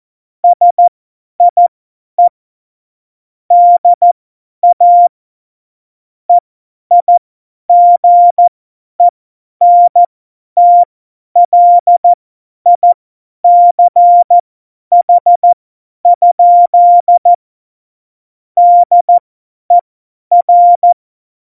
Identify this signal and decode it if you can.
Morse code, heard as a keyed tone, about 14 wpm, spaced as for 10 wpm, 700 Hz. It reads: SIE DA EIGENTLICH? DER
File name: Morsekurs / Witze / Witz 09. Januar 2018, 50 F70